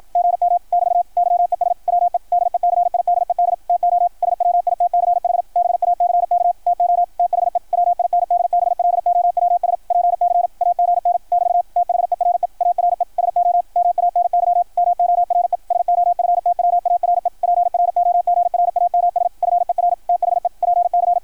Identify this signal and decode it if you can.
Morse code, heard as a keyed tone, about 53 wpm, 690 Hz, with radio noise. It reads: GM <BT> YES WE DECIDED TO SWITCH BACK TO THE WINDBLOWS PC AGN <BT> THERE ARE SO MANY MORE SOFTWARE PROGRAMS FER THE PC
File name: archived / CRSnet